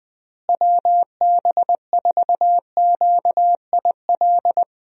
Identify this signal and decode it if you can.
Morse code, heard as a keyed tone, about 20 wpm, 700 Hz. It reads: WB4QIL
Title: USA Call Signs - 6 Characters - Repeated 3x - 20wpm